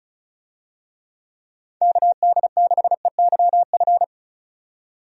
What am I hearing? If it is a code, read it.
Morse code, heard as a keyed tone, about 34 wpm, 700 Hz. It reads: KD6EYF
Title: USA Call Signs - 6 Characters - 35wpm